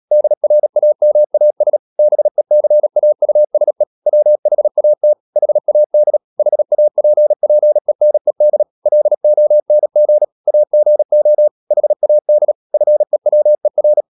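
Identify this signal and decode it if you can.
Morse code, heard as a keyed tone, about 37 wpm, 600 Hz. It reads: DRAMAS BECAUSE WHAT HAD HAPPENED LONG AGO HAD FEWER
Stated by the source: Morse code practice at 35–39 wpm sending English text of Ken Follett's novel Never, from CW med Gnister.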